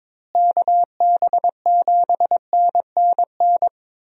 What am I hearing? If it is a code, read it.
Morse code, heard as a keyed tone, about 22 wpm, 700 Hz. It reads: KB7NNN
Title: USA Call Signs - 6 Characters - Repeated 2x - 22wpm